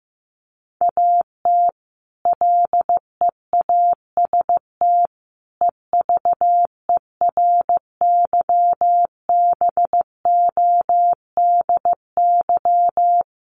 Morse code, heard as a keyed tone, about 15 wpm, 700 Hz. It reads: AT LEAST EVERYBODY
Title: Sentences from Top 1400 Words 15wpm